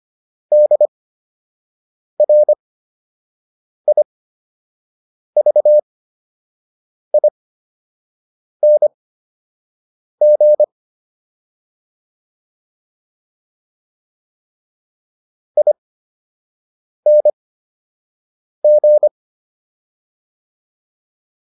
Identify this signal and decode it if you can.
Morse code, heard as a keyed tone, about 25 wpm, spaced as for 6 wpm, 600 Hz. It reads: DRIVING ING